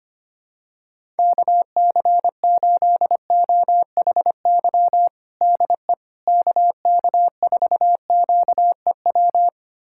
Morse code, heard as a keyed tone, about 25 wpm, 700 Hz. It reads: KC8OHY DE KK4QEW
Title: USA Callsign DE Callsign 25wpm